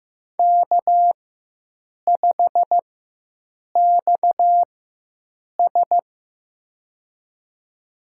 Morse code, heard as a keyed tone, about 15 wpm, spaced as for 7 wpm, 700 Hz. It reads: K5XS
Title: USA Call Signs - 4x Character Spacing - 15wpm